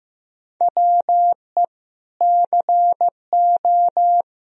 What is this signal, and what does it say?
Morse code, heard as a keyed tone, about 16 wpm, 700 Hz. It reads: WE CO
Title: Sentences from Top 1000 Words - Review Part 2 - 15wpm